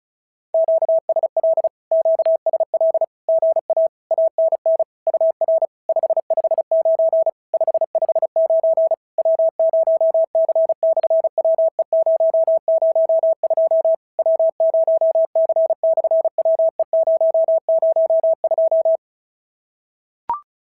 Morse code, heard as a keyed tone, about 33 wpm, 650 Hz. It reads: QSL QSL GA ANN UR 559 559 W0C/WE002 W0C/WE002
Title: SOTA QSO Practice 35wpm